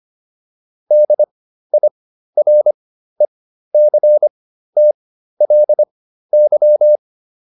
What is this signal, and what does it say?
Morse code, heard as a keyed tone, about 25 wpm, spaced as for 13 wpm, 600 Hz. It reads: DIRECTLY